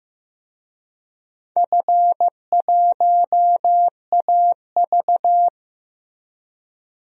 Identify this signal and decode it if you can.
Morse code, heard as a keyed tone, about 15 wpm, 700 Hz. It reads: F1AV
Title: DXCC-100 Call Signs - 4 Characters - Repeated 3x - 15wpm